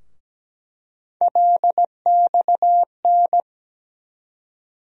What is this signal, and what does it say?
Morse code, heard as a keyed tone, about 17 wpm, 700 Hz. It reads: LXN
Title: USA Call Sign Postfixes - 3 Letters 17wpm